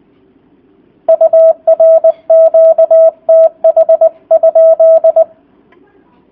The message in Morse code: UR QTH?

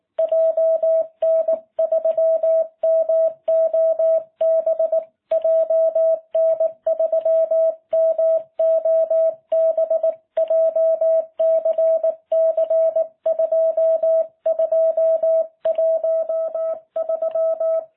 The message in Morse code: JN3MOB JN3MOB JCC 2213

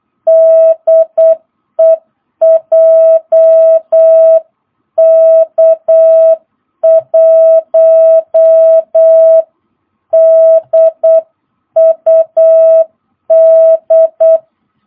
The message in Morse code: DE JK1DUD

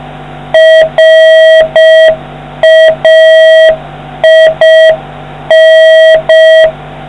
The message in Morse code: RAIN

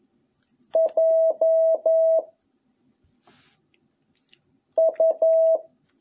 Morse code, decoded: J U

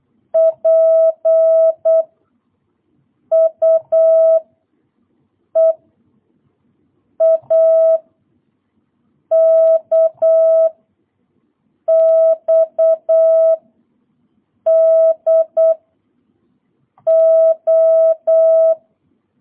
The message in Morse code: P U E A K X D O

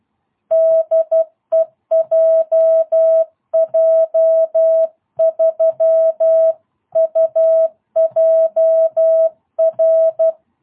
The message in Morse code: DE JJ3UJR

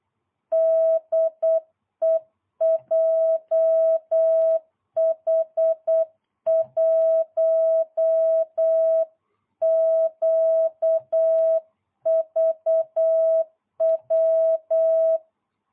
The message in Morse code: DE JH1QVW

